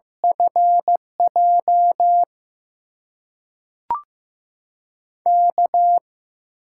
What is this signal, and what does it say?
Morse code, heard as a keyed tone, about 15 wpm, 700 Hz. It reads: FJ K
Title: USA Call Signs - 6 Characters - 8x Character Spacing - 15wpm